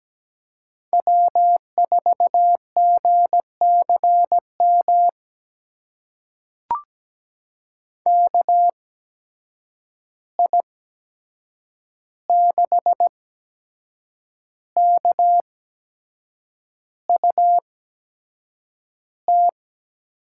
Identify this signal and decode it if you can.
Morse code, heard as a keyed tone, about 17 wpm, spaced as for 7 wpm, 700 Hz. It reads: W4GCM KI6KUT
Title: USA Call Signs - 8x Character Spacing - 17wpm